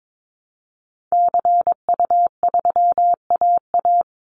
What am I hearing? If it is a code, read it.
Morse code, heard as a keyed tone, about 22 wpm, 700 Hz. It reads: CU3AA
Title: Call Signs Repeated 3x - 22wpm